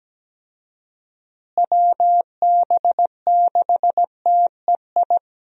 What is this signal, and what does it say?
Morse code, heard as a keyed tone, about 17 wpm, 700 Hz. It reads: WB6TEI
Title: USA Call Signs - 6 Characters - Repeated 2x - 17wpm